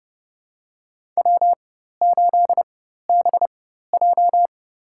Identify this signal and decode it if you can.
Morse code, heard as a keyed tone, about 30 wpm, spaced as for 14 wpm, 700 Hz. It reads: W8BJ
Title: USA Call Signs - 4 Characters - 4x Character Spacing - 30wpm